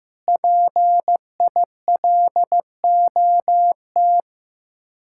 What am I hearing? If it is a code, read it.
Morse code, heard as a keyed tone, about 15 wpm, 700 Hz. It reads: PILOT
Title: Speed-Racing Common QSO Phrases Encore - 15wpm